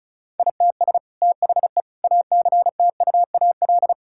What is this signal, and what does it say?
Morse code, heard as a keyed tone, about 34 wpm, 700 Hz. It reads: ITS THE ACTUAL
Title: Sentences from Top 1000 Words - Review Part 5 - 35wpm